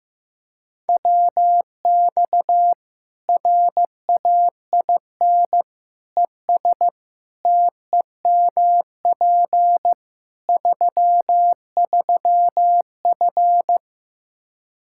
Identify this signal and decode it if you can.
Morse code, heard as a keyed tone, about 16 wpm, 700 Hz. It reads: WX RAIN ES TEMP 33F
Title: Practice QSO - Exchange 2 - Info - 15wpm